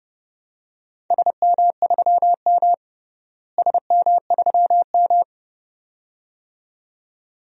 Morse code, heard as a keyed tone, about 29 wpm, 700 Hz. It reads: SM3M SM3M
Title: DXCC-100 Call Signs - 4 Characters - Repeated 2x - 30wpm